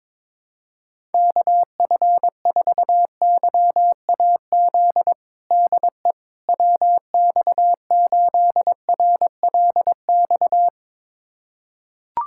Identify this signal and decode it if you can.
Morse code, heard as a keyed tone, about 22 wpm, 700 Hz. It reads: KF4YAZ DE WX8RLX
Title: USA Callsign DE Callsign 22wpm